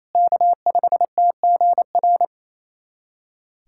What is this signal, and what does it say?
Morse code, heard as a keyed tone, about 28 wpm, 700 Hz. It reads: K5TGR
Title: USA Callsign DE Callsign - 5 Characters - 28wpm